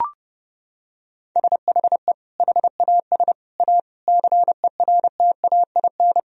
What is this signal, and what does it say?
Morse code, heard as a keyed tone, about 30 wpm, 700 Hz. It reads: SHE HAS A CERTAIN
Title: Sentences from Top 1000 Words - Review Part 4 - 30wpm